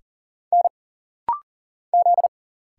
Morse code, heard as a keyed tone, about 40 wpm, 700 Hz. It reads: N Z
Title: Single Letter-Number Alt - Rapid-Fire 40wpm